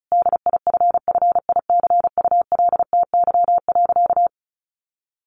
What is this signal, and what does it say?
Morse code, heard as a keyed tone, about 34 wpm, 700 Hz. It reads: DIFFICULTY.
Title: Sentences from Top 1500 Words 35wpm